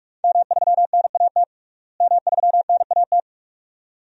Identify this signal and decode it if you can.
Morse code, heard as a keyed tone, about 41 wpm, 700 Hz. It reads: M3NAT M3NAT
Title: DXCC-100 Call Signs - 5 Characters - Repeated 2x - 45wpm